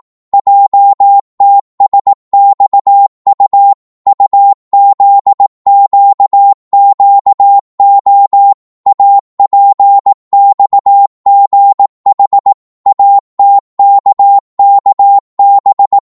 Morse code, heard as a keyed tone, about 18 wpm, 800 Hz. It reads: JTSXU UZQQO APXGH ATKKB